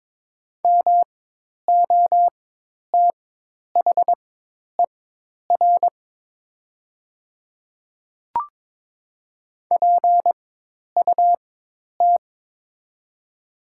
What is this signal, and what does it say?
Morse code, heard as a keyed tone, about 22 wpm, spaced as for 10 wpm, 700 Hz. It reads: MOTHER PUT
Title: 4x Character Spacing - Top 300 Words - 22wpm